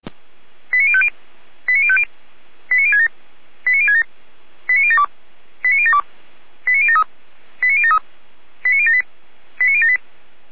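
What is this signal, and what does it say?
Five-tone selective call (ZVEI), heard as quick runs of five tones, tones of about 70 ms. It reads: 89050 89050 89067 89067 89072 89072 89073 89073 89078 89078
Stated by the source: ZVEI five-tone selective call received on SDR.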